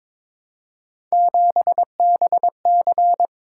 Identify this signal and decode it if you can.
Morse code, heard as a keyed tone, about 23 wpm, 700 Hz. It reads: 7BC
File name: practice / 22wpm / Three Letter-Number Combinations - 22wpm